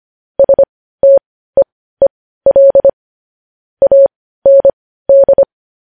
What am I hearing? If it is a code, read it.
Morse code, heard as a keyed tone, about 25 wpm, spaced as for 15 wpm, 550 Hz. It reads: STEEL AND
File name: podcast / News headlines-1x-25x15